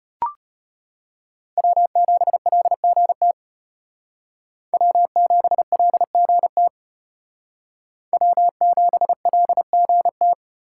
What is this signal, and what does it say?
Morse code, heard as a keyed tone, about 32 wpm, 700 Hz. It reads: W7LGT W7LGT W7LGT